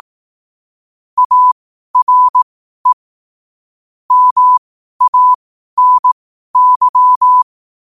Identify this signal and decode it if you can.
Morse code, heard as a keyed tone, about 18 wpm, spaced as for 12 wpm, 1000 Hz. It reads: ARE MANY
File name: 12wpm